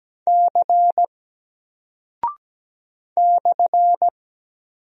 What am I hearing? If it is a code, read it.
Morse code, heard as a keyed tone, about 17 wpm, 700 Hz. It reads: C /
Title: Single Character (Letter:Number:Punctuation:Prosign) 17wpm